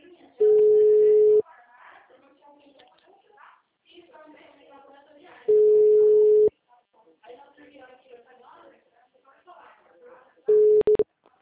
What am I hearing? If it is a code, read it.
Ringback tone (Europe, 425 Hz), with speech.